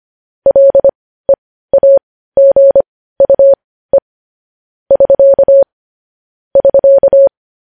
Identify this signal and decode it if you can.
Morse code, heard as a keyed tone, about 25 wpm, spaced as for 15 wpm, 550 Hz. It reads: LEAGUE <SK> <SK>
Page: news-headlines-in-morse-code-at-15-wpm